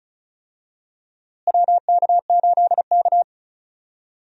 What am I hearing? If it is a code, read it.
Morse code, heard as a keyed tone, about 34 wpm, 700 Hz. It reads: WK8K
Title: USA Call Signs - 4 Characters - Repeated 2x - 35wpm